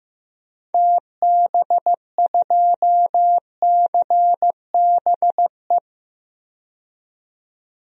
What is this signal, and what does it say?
Morse code, heard as a keyed tone, about 15 wpm, 700 Hz. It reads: TB2CBE
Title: DXCC-100 Call Signs - 6 Characters - 15wpm